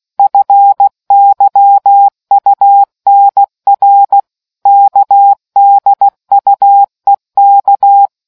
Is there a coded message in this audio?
Yes: Morse code from 0.2 to 8.1 s, heard as a keyed tone, about 16 wpm, 800 Hz. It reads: FYUNR KDUEK